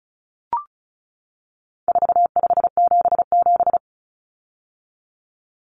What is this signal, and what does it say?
Morse code, heard as a keyed tone, about 34 wpm, 700 Hz. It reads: 4577
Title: Common Cut Numbers - 4-Digit 35wpm